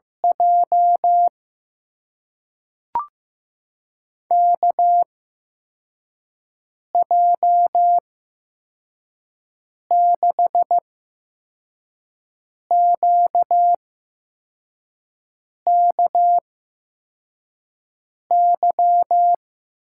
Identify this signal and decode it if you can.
Morse code, heard as a keyed tone, about 15 wpm, spaced as for 4 wpm, 700 Hz. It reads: J KJ6QKY
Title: USA Call Signs - 6 Characters - 8x Character Spacing - 15wpm